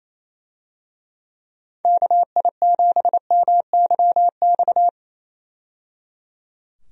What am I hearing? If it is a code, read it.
Morse code, heard as a keyed tone, about 28 wpm, 700 Hz. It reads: KI7MYX
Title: USA Call Signs - 6 Characters - 28wpm